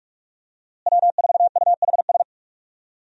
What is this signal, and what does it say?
Morse code, heard as a keyed tone, about 46 wpm, 700 Hz. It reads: W4UHS
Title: USA Call Signs - 5 Characters - Repeated 2x - 45wpm